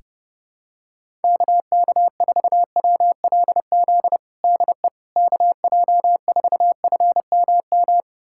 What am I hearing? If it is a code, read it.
Morse code, heard as a keyed tone, about 30 wpm, 700 Hz. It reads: KK4WLZ DE KJ4FMM
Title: USA Callsign DE Callsign - 6 Characters - 30wpm